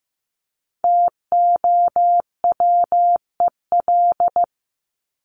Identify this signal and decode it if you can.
Morse code, heard as a keyed tone, about 15 wpm, 700 Hz. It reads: TOWEL
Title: Top 1501-1750 Words 15wpm